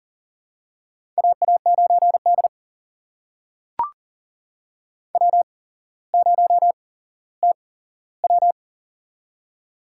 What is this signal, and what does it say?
Morse code, heard as a keyed tone, about 40 wpm, spaced as for 17 wpm, 700 Hz. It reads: AA9D W0TW
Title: USA Call Signs - 4 Characters - 8x Character Spacing - 40wpm